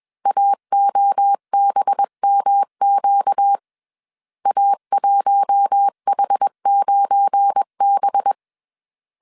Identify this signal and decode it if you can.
Morse code, heard as a keyed tone, about 21 wpm, 800 Hz. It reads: AOBMQ A1H9B
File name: CW / DET / 1 9 1500C S C 20wpm 2310